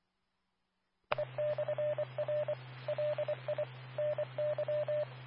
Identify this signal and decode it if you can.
Morse code, heard as a keyed tone, about 25 wpm, 600 Hz. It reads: E/R LI NY
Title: Nassau and Suffolk County Amateur Repeaters